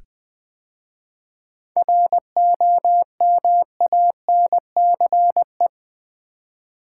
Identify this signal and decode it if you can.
Morse code, heard as a keyed tone, about 20 wpm, 700 Hz. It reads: ROMANCE
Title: 7 Letter Words - First 3 Letters Given - 20wpm